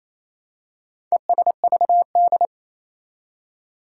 Morse code, heard as a keyed tone, about 27 wpm, 700 Hz. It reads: ESVD
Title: Four Challenging Letter-Number Combinations 28wpm